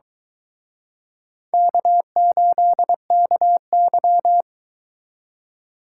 Morse code, heard as a keyed tone, about 24 wpm, 700 Hz. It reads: K8KY